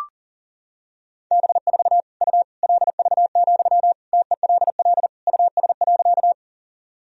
Morse code, heard as a keyed tone, about 40 wpm, 700 Hz. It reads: B4 U LV, TELL US.
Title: Sentences from Top 400 Words and Abbreviations 40wpm